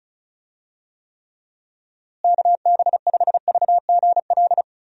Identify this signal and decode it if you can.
Morse code, heard as a keyed tone, about 34 wpm, 700 Hz. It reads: KB5VGL
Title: USA Call Signs - 6 Characters - Repeated 3x - 35wpm